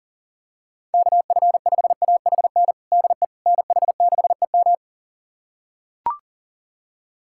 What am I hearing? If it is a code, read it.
Morse code, heard as a keyed tone, about 37 wpm, 700 Hz. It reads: KF5AHN DE NH6EM
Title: USA Callsign DE Callsign 40wpm